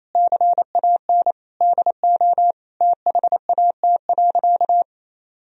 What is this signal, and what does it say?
Morse code, heard as a keyed tone, about 28 wpm, 700 Hz. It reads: CAN DO THAT.
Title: Sentences from Top 1000 Words - Review Part 1 - 28wpm